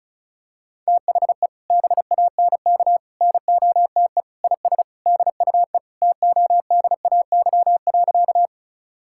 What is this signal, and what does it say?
Morse code, heard as a keyed tone, about 34 wpm, 700 Hz. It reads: THE BANK NOTE IS DUE TODAY.
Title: Sentences from Top 1000 Words - Review Part 6 - 35wpm